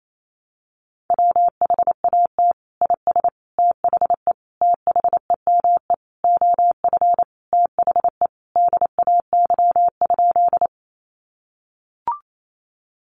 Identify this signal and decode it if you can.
Morse code, heard as a keyed tone, about 28 wpm, 700 Hz. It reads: WHAT IS THE THEME OF THE DAY?
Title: Sentences from Top 1100 Words 28wpm